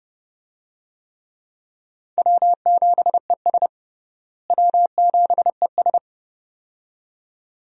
Morse code, heard as a keyed tone, about 30 wpm, 700 Hz. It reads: W7ES W7ES
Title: USA Call Signs - 4 Characters - Repeated 2x - 30wpm